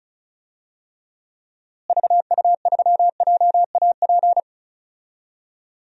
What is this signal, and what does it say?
Morse code, heard as a keyed tone, about 36 wpm, 700 Hz. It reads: VU3JAP
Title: DXCC-100 Call Signs - 6 Characters - Repeated 3x - 35wpm